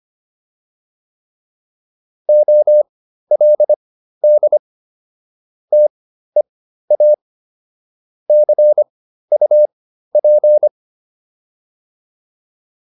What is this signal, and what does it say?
Morse code, heard as a keyed tone, about 25 wpm, spaced as for 13 wpm, 600 Hz. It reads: OLD TEA CUP